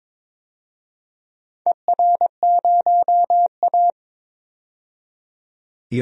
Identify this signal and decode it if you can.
Morse code, heard as a keyed tone, about 21 wpm, 700 Hz. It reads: ER0A
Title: DXCC-100 Call Signs - 4 Characters - 22wpm